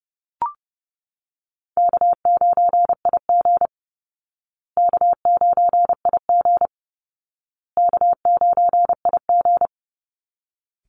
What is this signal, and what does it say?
Morse code, heard as a keyed tone, about 30 wpm, 700 Hz. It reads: K9IG K9IG K9IG
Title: Easy Call Signs Repeated 3x - 30wpm